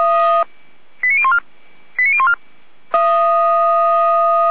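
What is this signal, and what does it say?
Five-tone selective call (ZVEI), heard as quick runs of five tones, tones of about 70 ms. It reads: 89014 89014
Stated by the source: ZVEI five-tone call 89001 received by SDR, from the fire brigade archive of Feuerwehr Orlamünde.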